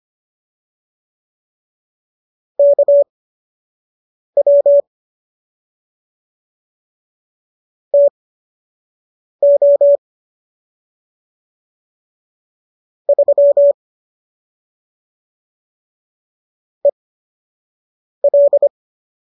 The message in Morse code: KW TO 3 EL

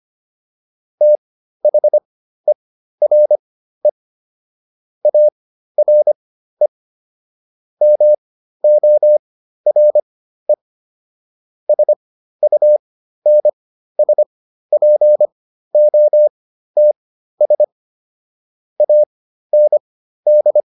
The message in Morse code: THERE ARE MORE SUNSPOTS AND

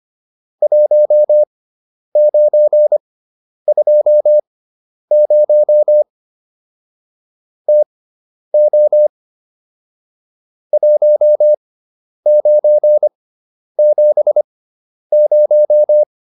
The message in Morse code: 1920 TO 1970

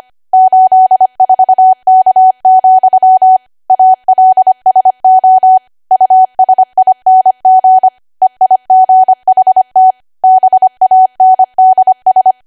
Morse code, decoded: 84K, ALSO USING EIGHT BANDS

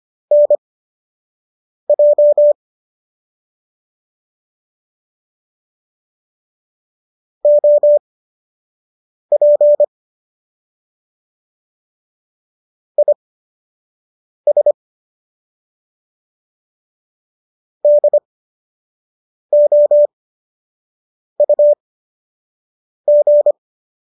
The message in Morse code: NJ OP IS DOUG

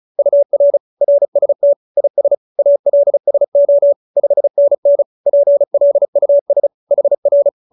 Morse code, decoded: UR RST IS ALSO 5NN PLUS HR

